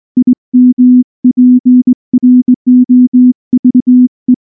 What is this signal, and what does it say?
Morse code, heard as a keyed tone, about 19 wpm, 250 Hz. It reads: IMPROVE
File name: morse-training-20WPM